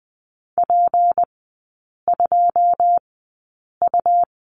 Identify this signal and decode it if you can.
Morse code, heard as a keyed tone, about 20 wpm, spaced as for 8 wpm, 700 Hz. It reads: P2U
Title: 152 Lesson 25e - Letter p - Three Characters 20wpm